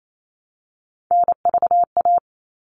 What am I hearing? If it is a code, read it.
Morse code, heard as a keyed tone, about 28 wpm, 700 Hz. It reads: NVA
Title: All Cut Numbers - 3-Digit 28wpm